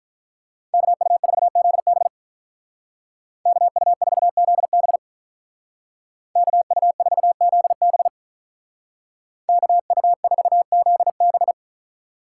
Morse code, heard as a keyed tone, about 42 wpm, 700 Hz. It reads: KU4ZB KU4ZB KU4ZB KU4ZB
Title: Speed Racing - USA 2x2 Call Signs 35wpm